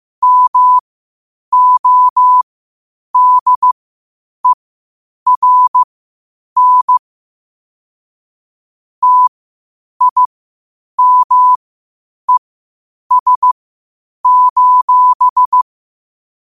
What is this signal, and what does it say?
Morse code, heard as a keyed tone, about 15 wpm, spaced as for 8 wpm, 1000 Hz. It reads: MODERN TIMES:
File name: qotdfiles / 8wpm